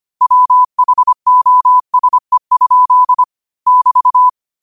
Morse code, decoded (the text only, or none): WHOSE? <BT>